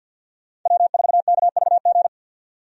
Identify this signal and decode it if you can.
Morse code, heard as a keyed tone, about 47 wpm, 700 Hz. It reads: W4KVG
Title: USA Call Signs - 5 Characters - 50wpm